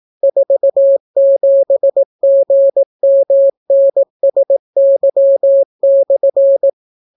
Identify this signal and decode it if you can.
Morse code, heard as a keyed tone, about 18 wpm, 550 Hz. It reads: 47GMNSY/